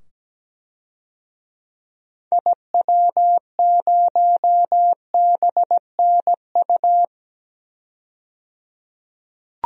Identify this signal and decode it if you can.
Morse code, heard as a keyed tone, about 17 wpm, 700 Hz. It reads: IW0BNU
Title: DXCC-100 Call Signs - 6 Characters - 17wpm